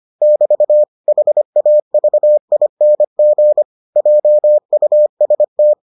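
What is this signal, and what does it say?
Morse code, heard as a keyed tone, about 25 wpm, 600 Hz. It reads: <BT> HAVING JUST
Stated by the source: Morse edition of the Foundations of Amateur Radio episodes.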